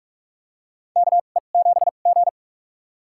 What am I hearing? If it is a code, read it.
Morse code, heard as a keyed tone, about 46 wpm, spaced as for 33 wpm, 700 Hz. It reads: KE7G
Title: USA Call Signs - 4 Characters - 2x Character Spacing - 45wpm